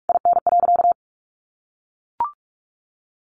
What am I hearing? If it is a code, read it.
Morse code, heard as a keyed tone, about 44 wpm, 700 Hz. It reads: IN.
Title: Sentences from Top 1300 Words 45wpm